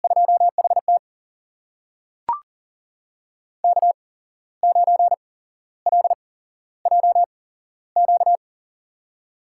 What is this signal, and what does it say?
Morse code, heard as a keyed tone, about 40 wpm, spaced as for 14 wpm, 700 Hz. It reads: 2HT K9LJQ
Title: USA Call Signs - 8x Character Spacing - 40wpm